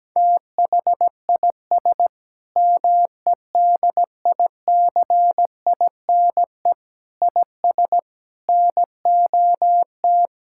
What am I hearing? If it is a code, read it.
Morse code, heard as a keyed tone, about 17 wpm, 700 Hz. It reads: THIS MEDICINE IS NOT
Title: Sentences from Top 1000 Words - Review Part 6 - 17wpm